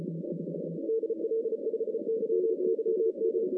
Morse code, heard as a keyed tone, about 32 wpm, 440 Hz. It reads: DL5YYM
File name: 2015 cq ww cw / 2015 cqwww k5tr - 0-41